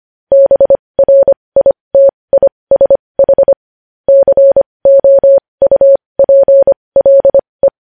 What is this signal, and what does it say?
Morse code, heard as a keyed tone, about 25 wpm, spaced as for 20 wpm, 550 Hz. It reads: BRITISH COUPLE
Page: News Headlines in Morse Code at 20 WPM